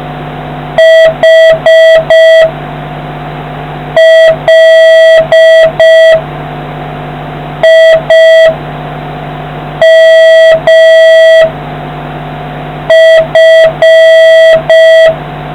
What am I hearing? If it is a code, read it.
Morse code, heard as a keyed tone, about 5 wpm, 1950 Hz, with noise. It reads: H L I M F